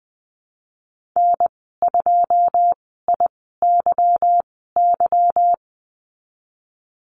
Morse code, heard as a keyed tone, about 19 wpm, spaced as for 14 wpm, 700 Hz. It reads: N2IYY
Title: 161 Lesson 26g - Letter g - Call Signs 20wpm